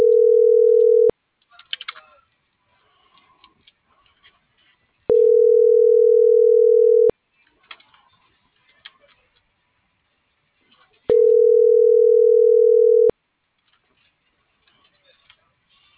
Ringback tone (North America), 440 + 480 Hz.